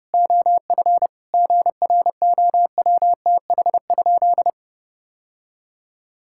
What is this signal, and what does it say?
Morse code, heard as a keyed tone, about 32 wpm, 700 Hz. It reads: OF GROWTH?
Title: Sentences from Top 1000 Words - Review Part 5 - 30wpm